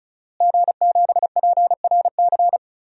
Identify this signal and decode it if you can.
Morse code, heard as a keyed tone, about 34 wpm, 700 Hz. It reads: G7PRC
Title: DXCC-100 Call Signs - 5 Characters - Repeated 2x - 35wpm